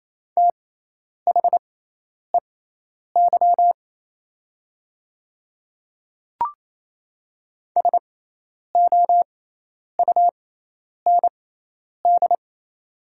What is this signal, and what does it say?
Morse code, heard as a keyed tone, about 27 wpm, spaced as for 10 wpm, 700 Hz. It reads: THEY SOUND
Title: 6x Character Spacing - Top 100 Words - 28wpm